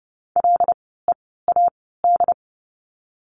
Morse code, heard as a keyed tone, about 30 wpm, spaced as for 17 wpm, 700 Hz. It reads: LEAD
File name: Lesson 16f - Letter c - Words 30wpm